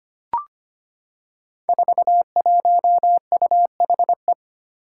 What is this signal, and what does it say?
Morse code, heard as a keyed tone, about 25 wpm, 700 Hz. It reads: 41UHE
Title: Five Challenging Letter-Number Combinations 25wpm